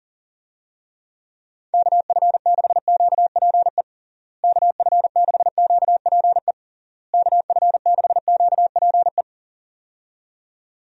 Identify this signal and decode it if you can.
Morse code, heard as a keyed tone, about 40 wpm, 700 Hz. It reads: KF6QPE KF6QPE KF6QPE
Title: USA Call Signs - 6 Characters - Repeated 3x - 40wpm